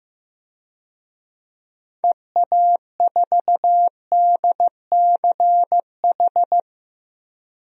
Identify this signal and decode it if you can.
Morse code, heard as a keyed tone, about 16 wpm, 700 Hz. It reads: EA4DCH
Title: DXCC-100 Call Signs - 6 Characters - Repeated 2x - 15wpm